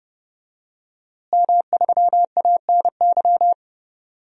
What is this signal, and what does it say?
Morse code, heard as a keyed tone, about 32 wpm, 700 Hz. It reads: M3ANY
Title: DXCC-100 Call Signs - 5 Characters - Repeated 3x - 30wpm